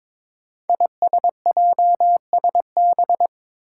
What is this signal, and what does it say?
Morse code, heard as a keyed tone, about 22 wpm, 700 Hz. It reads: ISJSB
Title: Five Challenging Letter-Number Combinations 22wpm